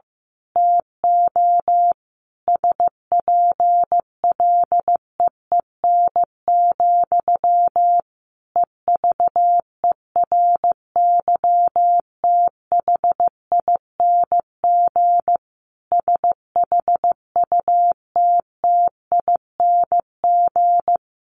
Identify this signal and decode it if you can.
Morse code, heard as a keyed tone, about 15 wpm, 700 Hz. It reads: TO SPLEEN, EVERYTHING SHUTTING
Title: A Sound Of Thunder 15wpm